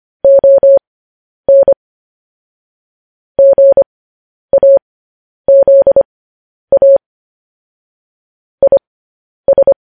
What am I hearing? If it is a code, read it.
Morse code, heard as a keyed tone, about 25 wpm, spaced as for 10 wpm, 550 Hz. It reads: ON GAZA IS
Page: news headlines in morse code at 10 wpm